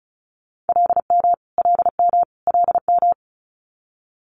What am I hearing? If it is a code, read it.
Morse code, heard as a keyed tone, about 36 wpm, 700 Hz. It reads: LM LM LM
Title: Multi-letter Phonemes Repeated 3x 35wpm